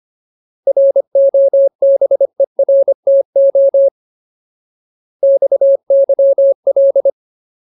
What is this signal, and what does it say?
Morse code, heard as a keyed tone, about 25 wpm, 550 Hz. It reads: ROBERTO XYL
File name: Ultimate Word List / 25WPM Ultimate